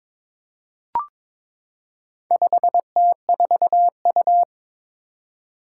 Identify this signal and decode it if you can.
Morse code, heard as a keyed tone, about 21 wpm, 700 Hz. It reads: 5T4U
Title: Four Challenging Letter-Number Combinations 22wpm